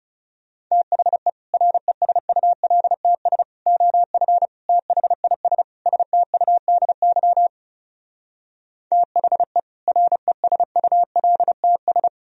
Morse code, heard as a keyed tone, about 40 wpm, 700 Hz. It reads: THE RESULTS OF THIS STUDY THE RESULTS